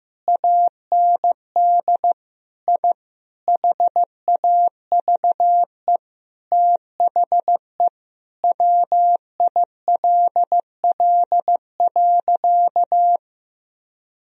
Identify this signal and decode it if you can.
Morse code, heard as a keyed tone, about 15 wpm, 700 Hz. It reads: AND I HAVE THE WILL.
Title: Sentences from Top 1000 Words - Review Part 2 - 15wpm